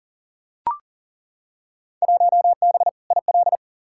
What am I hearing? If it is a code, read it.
Morse code, heard as a keyed tone, about 40 wpm, 700 Hz. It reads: 1B IL
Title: Field Day Exchanges - 40wpm